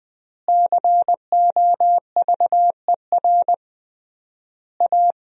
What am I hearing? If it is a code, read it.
Morse code, heard as a keyed tone, about 21 wpm, 700 Hz. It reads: COVER A
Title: 3x Word-Spacing - Sentences from Top 500 Words - 20wpm